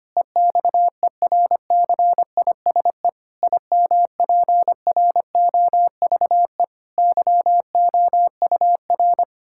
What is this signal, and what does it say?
Morse code, heard as a keyed tone, about 25 wpm, 700 Hz. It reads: EXERCISE IMPROVE YOUR
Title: Sentences from Top 1000 Words - Review Part 6 - 25wpm